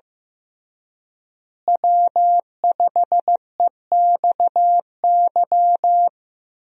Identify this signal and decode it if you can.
Morse code, heard as a keyed tone, about 15 wpm, 700 Hz. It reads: W5EXY
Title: USA Call Signs - 5 Characters - Repeated 2x - 15wpm